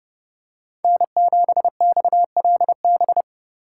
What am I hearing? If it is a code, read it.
Morse code, heard as a keyed tone, about 30 wpm, 700 Hz. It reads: N7XLB
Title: USA Call Signs - 5 Characters - 3x Character Spacing - 30wpm